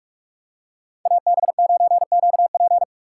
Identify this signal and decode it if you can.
Morse code, heard as a keyed tone, about 48 wpm, 700 Hz. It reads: AB9QP
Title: USA Call Signs - 5 Characters - 6x Character Spacing - 45wpm